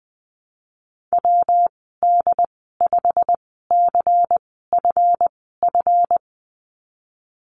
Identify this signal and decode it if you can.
Morse code, heard as a keyed tone, about 20 wpm, spaced as for 15 wpm, 700 Hz. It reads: WD5CFF
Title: 147 Lesson 24g - Letter y - Call Signs 20wpm